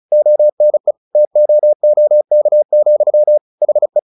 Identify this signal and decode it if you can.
Morse code, heard as a keyed tone, about 34 wpm, 600 Hz. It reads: ONE TOOK, HE
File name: the old man and the sea 2 35wpm